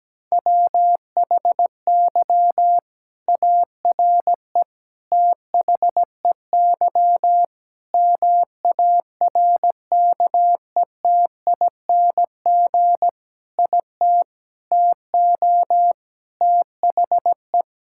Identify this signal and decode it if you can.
Morse code, heard as a keyed tone, about 17 wpm, 700 Hz. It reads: WHY ARE THEY MARKETING IT TO THE
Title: Sentences from Top 1000 Words - Review Part 5 - 17wpm